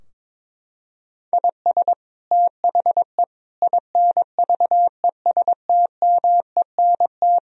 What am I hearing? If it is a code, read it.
Morse code, heard as a keyed tone, about 21 wpm, 700 Hz. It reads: IS THE INVESTMENT
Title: Sentences from Top 1000 Words - Review Part 4 - 22wpm